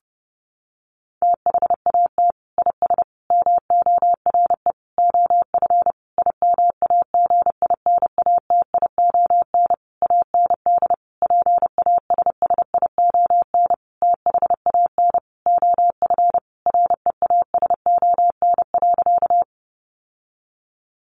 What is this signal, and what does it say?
Morse code, heard as a keyed tone, about 30 wpm, 700 Hz. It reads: THAT IS MORE OF IMAGINATION AND PASSION THAN OF REASON.
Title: Sentences from Top 1300 Words 30wpm